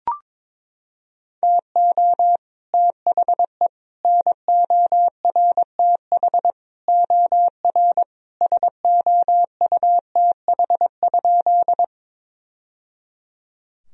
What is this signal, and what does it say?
Morse code, heard as a keyed tone, about 21 wpm, 700 Hz. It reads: TO THE NORTH OR SOUTH?
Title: Sentences from Top 1000 Words - Review Part 2 - 22wpm